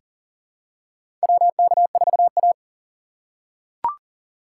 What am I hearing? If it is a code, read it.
Morse code, heard as a keyed tone, about 40 wpm, 700 Hz. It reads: WK4A
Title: USA Call Signs - 4 Characters - 3x Character Spacing - 40wpm